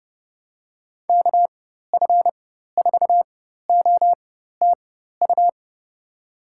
Morse code, heard as a keyed tone, about 30 wpm, spaced as for 14 wpm, 700 Hz. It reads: KF4OTU
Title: USA Call Signs - 6 Characters - 4x Character Spacing - 30wpm